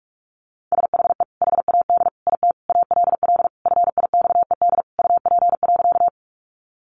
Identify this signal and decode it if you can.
Morse code, heard as a keyed tone, about 44 wpm, 700 Hz. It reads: SHE HAD IT ALL FIXED UP.
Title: Sentences from Top 1100 Words 45wpm